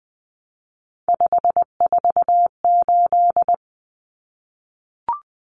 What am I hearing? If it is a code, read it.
Morse code, heard as a keyed tone, about 20 wpm, 700 Hz. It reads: 548
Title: Numbers - 3-Digit 20wpm